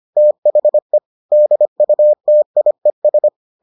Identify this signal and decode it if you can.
Morse code, heard as a keyed tone, about 25 wpm, 600 Hz. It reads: THE DUTIES